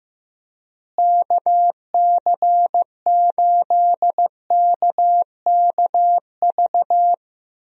Morse code, heard as a keyed tone, about 15 wpm, 700 Hz. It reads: KC8KKV
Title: USA Call Signs - 6 Characters - Repeated 2x - 15wpm